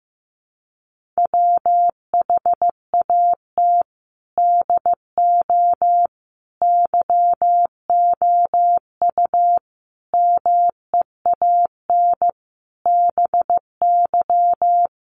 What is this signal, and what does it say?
Morse code, heard as a keyed tone, about 15 wpm, 700 Hz. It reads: WHAT DO YOU MEAN BY
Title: Sentences from Top 200 Words 15wpm